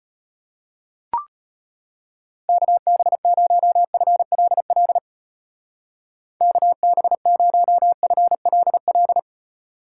Morse code, heard as a keyed tone, about 30 wpm, 700 Hz. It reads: KB0FLL KB0FLL
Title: Speed Racing - USA 2x3 Call Signs 25wpm